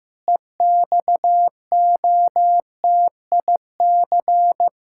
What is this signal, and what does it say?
Morse code, heard as a keyed tone, about 16 wpm, 700 Hz. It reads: EXOTIC
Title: Pangram Challenge v2 - 15wpm